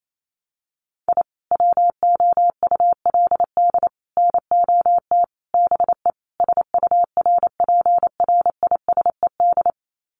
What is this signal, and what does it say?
Morse code, heard as a keyed tone, about 28 wpm, 700 Hz. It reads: I WOULD NOT BE SURPRISED
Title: Sets of 5 Words Encore2 28wpm